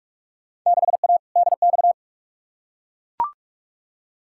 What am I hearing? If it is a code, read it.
Morse code, heard as a keyed tone, about 43 wpm, 700 Hz. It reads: 6A DX
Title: Field Day Exchanges Encore - 45wpm